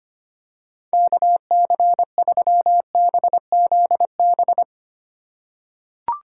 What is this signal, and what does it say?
Morse code, heard as a keyed tone, about 25 wpm, 700 Hz. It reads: KC3BZB
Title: USA Call Signs - 6 Characters - 4x Character Spacing - 25wpm